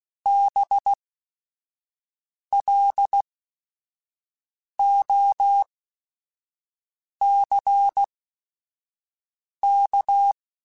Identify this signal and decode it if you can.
Morse code, heard as a keyed tone, about 16 wpm, spaced as for 5 wpm, 800 Hz. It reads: BLOCK
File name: General / General-mp3s / Houses @5